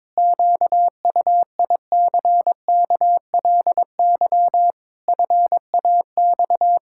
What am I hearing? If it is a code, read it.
Morse code, heard as a keyed tone, about 22 wpm, 700 Hz. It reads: QUICKLY FAX
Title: Pangram Challenge v2 - 22wpm